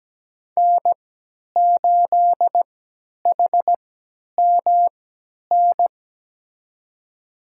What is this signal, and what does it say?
Morse code, heard as a keyed tone, about 17 wpm, spaced as for 10 wpm, 700 Hz. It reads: N8HMN
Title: USA Call Signs - 5 Characters - 3x Character Spacing - 17wpm